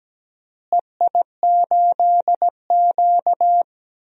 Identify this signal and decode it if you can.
Morse code, heard as a keyed tone, about 17 wpm, 700 Hz. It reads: EI8Q
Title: DXCC-100 Call Signs - 4 Characters - Repeated 2x - 17wpm